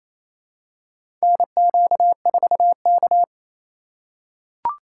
Morse code, heard as a keyed tone, about 28 wpm, 700 Hz. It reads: NQ4K
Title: USA Call Signs - 4 Characters - 2x Character Spacing - 28wpm